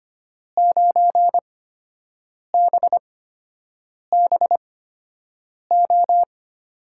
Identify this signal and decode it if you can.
Morse code, heard as a keyed tone, about 25 wpm, spaced as for 7 wpm, 700 Hz. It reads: 9BBO
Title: USA Call Signs - 5 Characters - 8x Character Spacing - 25wpm